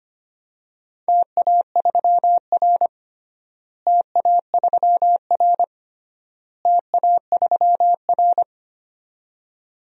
Morse code, heard as a keyed tone, about 25 wpm, 700 Hz. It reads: TA3R TA3R TA3R
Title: DXCC-100 Call Signs - 4 Characters - Repeated 3x - 25wpm